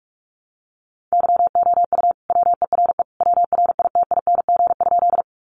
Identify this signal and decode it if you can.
Morse code, heard as a keyed tone, about 44 wpm, 700 Hz. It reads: YOU WERE WRITING?
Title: Sentences from Top 800 Words 45wpm